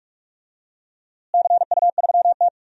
Morse code, heard as a keyed tone, about 42 wpm, 700 Hz. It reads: CU3T
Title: DXCC-100 Call Signs - 4 Characters - Repeated 2x - 45wpm